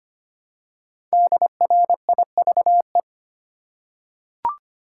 Morse code, heard as a keyed tone, about 25 wpm, 700 Hz. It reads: DRIVE
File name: 5 Letter Words - First 3 Letters Given - 25wpm